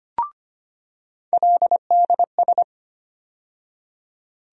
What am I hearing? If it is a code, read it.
Morse code, heard as a keyed tone, about 25 wpm, 700 Hz. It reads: LDS
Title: USA Call Sign Postfixes - 3 Letters 25wpm